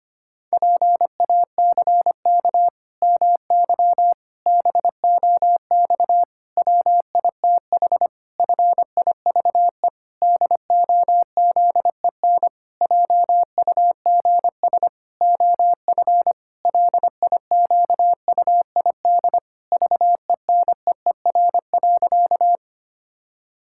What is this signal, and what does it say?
Morse code, heard as a keyed tone, about 25 wpm, 700 Hz. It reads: PACK MY BOX WITH FIVE DOZEN JUGS OF LIQUID VENEER.
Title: Pangram Challenge v2 - 25wpm